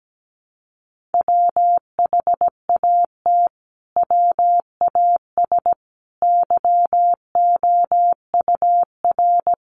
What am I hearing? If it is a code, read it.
Morse code, heard as a keyed tone, about 17 wpm, 700 Hz. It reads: WHAT WAS YOUR
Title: Sentences from Top 1500 Words 17wpm